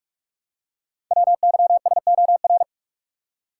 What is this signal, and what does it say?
Morse code, heard as a keyed tone, about 44 wpm, 700 Hz. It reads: WYSOR
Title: Code Groups - 5 Letters - 45wpm